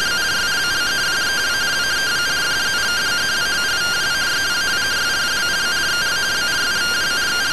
Two-tone teleprinter signal, about 170 Hz shift.
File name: findings / MR AXIAL 0mm 6mm.gen-12 rtty rx